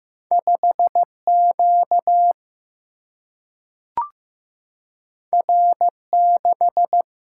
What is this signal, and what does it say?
Morse code, heard as a keyed tone, about 15 wpm, 700 Hz. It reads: 5 Q R 6
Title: Single-Letter Single-Number 15wpm